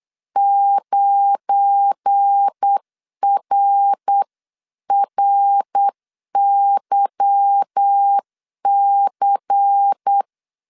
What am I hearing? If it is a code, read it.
Morse code, heard as a keyed tone, about 8 wpm, 800 Hz. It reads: 9RRYC
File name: CW / DET / 1 9 1500C S A 8wpm 5637